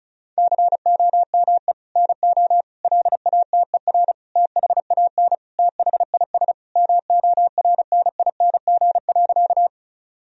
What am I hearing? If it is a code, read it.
Morse code, heard as a keyed tone, about 34 wpm, 700 Hz. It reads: COME NO LATER THAN THIS MORNING.
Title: Sentences from Top 1000 Words - Review Part 4 - 35wpm